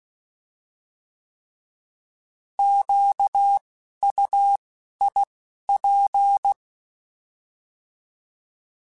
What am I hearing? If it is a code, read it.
Morse code, heard as a keyed tone, about 16 wpm, 800 Hz. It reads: Q U I P
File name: Alphabetical / Alphabetical mp3s / Qs @10